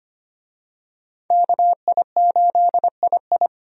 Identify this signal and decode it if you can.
Morse code, heard as a keyed tone, about 25 wpm, 700 Hz. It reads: KI8II
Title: USA Call Signs - 5 Characters - 25wpm